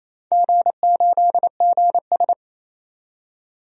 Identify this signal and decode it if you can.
Morse code, heard as a keyed tone, about 28 wpm, 700 Hz. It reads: G8GS
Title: DXCC-100 Call Signs - 4 Characters - 28wpm